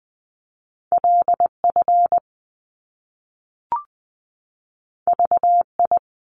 Morse code, heard as a keyed tone, about 20 wpm, 700 Hz. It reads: LF VI
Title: Top Two Letter Combinations in Words 20wpm